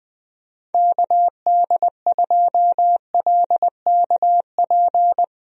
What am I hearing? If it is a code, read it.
Morse code, heard as a keyed tone, about 19 wpm, 700 Hz. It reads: KD2LKP
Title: USA Call Signs - 6 Characters - Repeated 3x - 20wpm